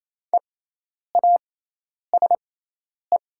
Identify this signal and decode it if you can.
Morse code, heard as a keyed tone, about 28 wpm, spaced as for 10 wpm, 700 Hz. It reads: EASE
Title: 6x Character Spacing - Top 300 Words - 28wpm